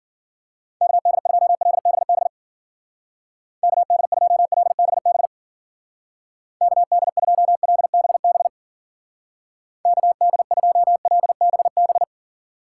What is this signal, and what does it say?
Morse code, heard as a keyed tone, about 52 wpm, 700 Hz. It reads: KD2LBB KD2LBB KD2LBB KD2LBB